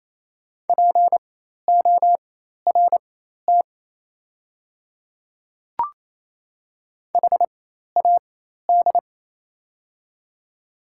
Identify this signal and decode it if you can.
Morse code, heard as a keyed tone, about 28 wpm, spaced as for 13 wpm, 690 Hz. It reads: PORT HAD
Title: 4x Character Spacing - Top 300 Words - 28wpm